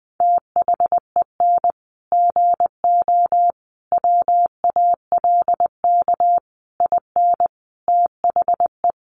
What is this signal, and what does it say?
Morse code, heard as a keyed tone, about 20 wpm, 700 Hz. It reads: THEN GO WALK IN THE
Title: Sentences from Top 1300 Words 20wpm